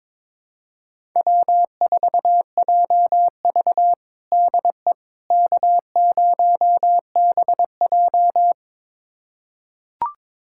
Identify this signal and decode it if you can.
Morse code, heard as a keyed tone, about 22 wpm, 700 Hz. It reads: W4JV DE K0BJ
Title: USA Callsign DE Callsign - 4 Characters - 22wpm